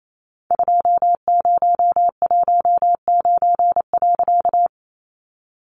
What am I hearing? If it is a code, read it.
Morse code, heard as a keyed tone, about 28 wpm, 700 Hz. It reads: 2019.
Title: A Sound Of Thunder 28wpm